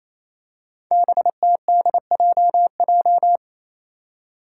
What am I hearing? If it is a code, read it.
Morse code, heard as a keyed tone, about 28 wpm, 700 Hz. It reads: BTDJJ
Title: Five Challenging Letter-Number Combinations 28wpm